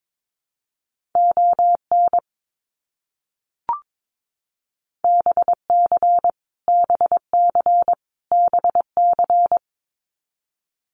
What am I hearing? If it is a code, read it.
Morse code, heard as a keyed tone, about 22 wpm, 700 Hz. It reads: ON BC BC BC
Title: Canadian Provinces and Territories - Repeated 3x - 22wpm